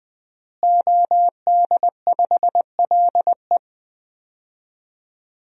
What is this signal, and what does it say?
Morse code, heard as a keyed tone, about 21 wpm, 700 Hz. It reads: OD5LE
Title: DXCC-100 Call Signs - 5 Characters - Repeated 2x - 20wpm